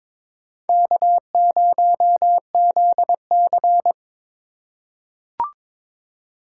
Morse code, heard as a keyed tone, about 22 wpm, 700 Hz. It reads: K0ZC
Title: USA Call Signs - 4 Characters - 2x Character Spacing - 22wpm